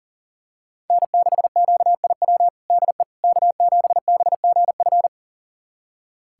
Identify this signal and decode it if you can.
Morse code, heard as a keyed tone, about 40 wpm, 700 Hz. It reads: N6QIW DE K7BGF
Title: USA Callsign DE Callsign - 5 Characters - 40wpm